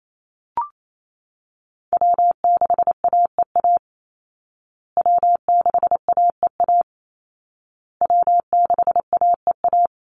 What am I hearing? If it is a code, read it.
Morse code, heard as a keyed tone, about 28 wpm, 700 Hz. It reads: W6AEA W6AEA W6AEA
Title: Call Signs Repeated 2x - 28wpm